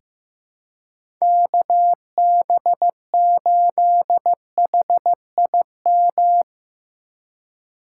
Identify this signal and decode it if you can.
Morse code, heard as a keyed tone, about 15 wpm, 700 Hz. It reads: KB8HIM
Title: USA Call Signs - 6 Characters - 6x Character Spacing - 15wpm